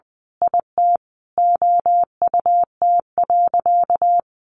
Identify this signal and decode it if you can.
Morse code, heard as a keyed tone, about 20 wpm, 700 Hz. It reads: IT OUT.
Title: Sentences from Top 1500 Words 20wpm